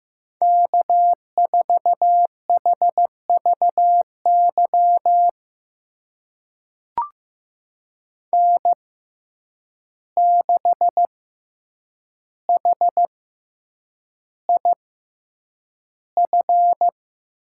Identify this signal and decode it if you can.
Morse code, heard as a keyed tone, about 15 wpm, spaced as for 8 wpm, 700 Hz. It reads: K4HVY N6HIF
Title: USA Call Signs - 5 Characters - 6x Character Spacing - 15wpm